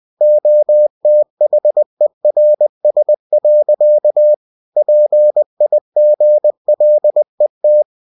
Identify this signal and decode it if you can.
Morse code, heard as a keyed tone, about 20 wpm, 600 Hz. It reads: OTHERS. PIGLET
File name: cw-books / winnie-the-pooh / winnie-20wpm-page-0085